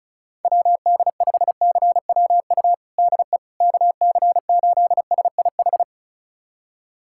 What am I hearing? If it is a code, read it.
Morse code, heard as a keyed tone, about 36 wpm, 700 Hz. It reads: WD5CWU DE KC8SIH
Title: USA Callsign DE Callsign 35wpm